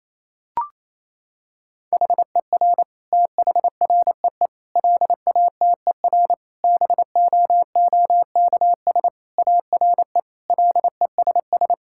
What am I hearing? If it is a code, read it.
Morse code, heard as a keyed tone, about 28 wpm, 700 Hz. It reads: HER THREE LATER BOOKS ARE LESS
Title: Sentences from Top 1000 Words - Review Part 6 - 28wpm